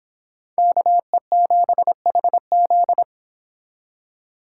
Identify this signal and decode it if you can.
Morse code, heard as a keyed tone, about 26 wpm, 700 Hz. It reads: KE7HZ
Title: Speed Racing - USA 2x2 Call Signs 22wpm